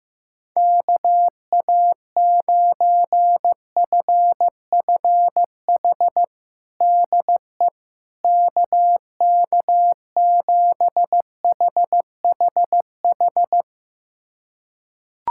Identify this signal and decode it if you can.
Morse code, heard as a keyed tone, about 15 wpm, 700 Hz. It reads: KA9FFH DE KK7HHH
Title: USA Callsign DE Callsign - 6 Characters - 15wpm